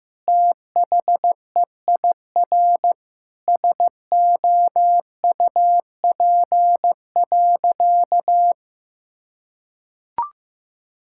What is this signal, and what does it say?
Morse code, heard as a keyed tone, about 16 wpm, 700 Hz. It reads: THEIR SOUP.
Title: Sentences from Top 1000 Words - Review Part 6 - 15wpm